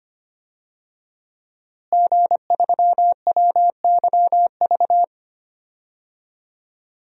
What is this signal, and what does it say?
Morse code, heard as a keyed tone, about 25 wpm, 700 Hz. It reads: G3WYV
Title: DXCC-100 Call Signs - 5 Characters - Repeated 3x - 25wpm